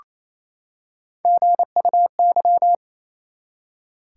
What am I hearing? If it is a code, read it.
Morse code, heard as a keyed tone, about 28 wpm, 700 Hz. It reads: GUY
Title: Three Letter Words - 28wpm